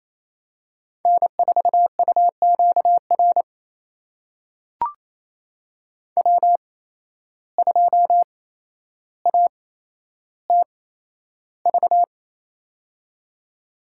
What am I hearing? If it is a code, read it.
Morse code, heard as a keyed tone, about 28 wpm, spaced as for 12 wpm, 700 Hz. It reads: N4UQR W2ATV
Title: USA Call Signs - 5 Characters - 8x Character Spacing - 28wpm